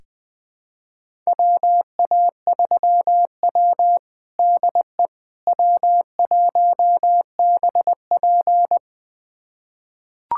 Morse code, heard as a keyed tone, about 20 wpm, 700 Hz. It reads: WA3W DE W1BP
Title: USA Callsign DE Callsign - 4 Characters - 20wpm